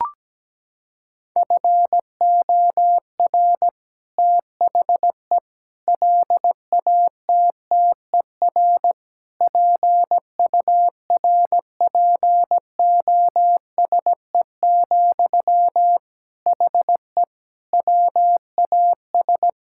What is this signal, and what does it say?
Morse code, heard as a keyed tone, about 17 wpm, 700 Hz. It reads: FOR THE LATTER PURPOSE, HE WAS
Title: Sentences from Top 1600 Words 17wpm